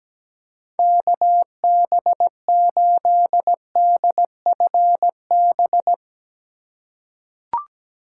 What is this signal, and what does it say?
Morse code, heard as a keyed tone, about 17 wpm, 700 Hz. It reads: KB8DFB
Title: USA Call Signs - 6 Characters - 3x Character Spacing - 17wpm